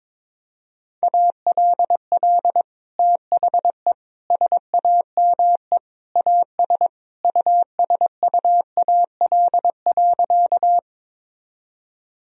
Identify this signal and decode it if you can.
Morse code, heard as a keyed tone, about 21 wpm, 700 Hz. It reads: ALL THE SAME AS USUAL.
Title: Sentences from Top 1000 Words - Review Part 2 - 22wpm